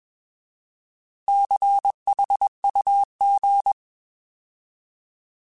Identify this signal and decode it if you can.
Morse code, heard as a keyed tone, about 21 wpm, 800 Hz. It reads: CHUG